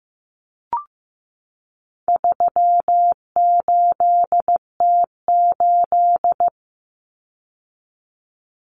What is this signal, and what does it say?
Morse code, heard as a keyed tone, about 15 wpm, 700 Hz. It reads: 38T8
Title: Common Cut Numbers - 4-Digit 15wpm